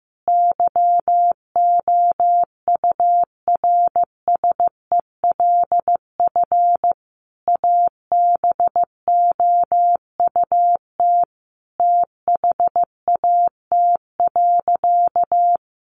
Morse code, heard as a keyed tone, about 15 wpm, 700 Hz. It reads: YOURSELF ABOUT THAT.
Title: Sentences from Top 1000 Words 15wpm